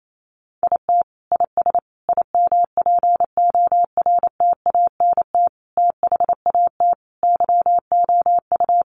Morse code, heard as a keyed tone, about 28 wpm, 700 Hz. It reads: IT IS IMPORTANT THAT YOU
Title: Sentences from Top 600 Words 28wpm